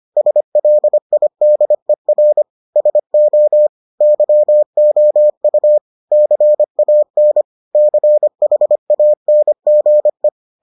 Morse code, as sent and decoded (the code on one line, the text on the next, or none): ... .-.. .. -.. .-. / ... --- / -.-- --- ..- / -.-. .- -. / -.-. .... .- -. --. .
SLIDER SO YOU CAN CHANGE